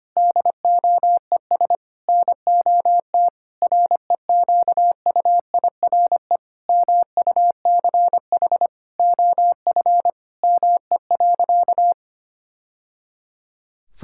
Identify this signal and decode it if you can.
Morse code, heard as a keyed tone, about 25 wpm, 700 Hz. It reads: DOES NOT REQUIRE MUCH OF ME.
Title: Sentences from Top 1000 Words - Review Part 6 - 25wpm